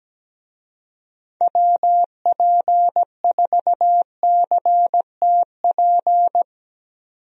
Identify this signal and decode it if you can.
Morse code, heard as a keyed tone, about 17 wpm, 700 Hz. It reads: WP4CTP
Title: USA Call Signs - 6 Characters - 17wpm